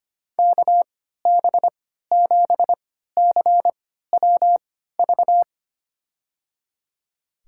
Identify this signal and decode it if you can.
Morse code, heard as a keyed tone, about 25 wpm, spaced as for 14 wpm, 700 Hz. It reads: KB7CWV
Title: USA Call Signs - 3x Character Spacing - 25wpm